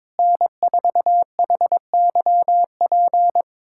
Morse code, heard as a keyed tone, about 22 wpm, 700 Hz. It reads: N4HYP
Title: USA Callsign DE Callsign - 5 Characters - 22wpm